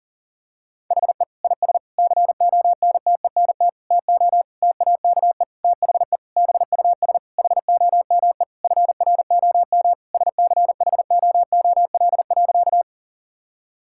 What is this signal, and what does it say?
Morse code, heard as a keyed tone, about 40 wpm, 700 Hz. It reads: HE IS CONTENT TO TAKE THE BUS HOME FROM SCHOOL.
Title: Sentences from Top 1000 Words - Review Part 6 - 40wpm